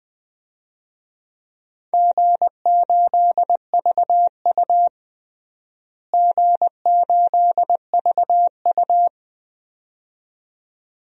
Morse code, heard as a keyed tone, about 20 wpm, 700 Hz. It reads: G8VU G8VU
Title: DXCC-100 Call Signs - 4 Characters - Repeated 2x - 20wpm